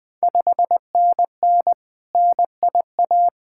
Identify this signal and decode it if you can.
Morse code, heard as a keyed tone, about 20 wpm, 700 Hz. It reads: 5NN NIA
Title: Ontario QSO Party Exchange 20wpm